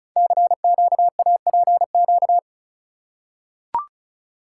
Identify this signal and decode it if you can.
Morse code, heard as a keyed tone, about 36 wpm, 700 Hz. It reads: CQAPQ
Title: Code Groups - 5 Letters - 35wpm